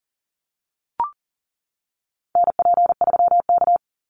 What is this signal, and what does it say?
Morse code, heard as a keyed tone, about 41 wpm, 700 Hz. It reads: NP3K
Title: Call Signs - 40wpm